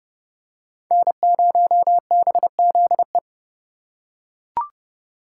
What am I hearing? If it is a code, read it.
Morse code, heard as a keyed tone, about 28 wpm, 700 Hz. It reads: N0BZE